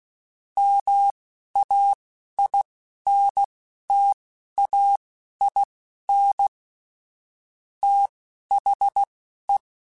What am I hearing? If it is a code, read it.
Morse code, heard as a keyed tone, about 16 wpm, spaced as for 11 wpm, 800 Hz. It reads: MAINTAIN THE
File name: Technical Subjects / Technical Subjects-mp3s / XC @10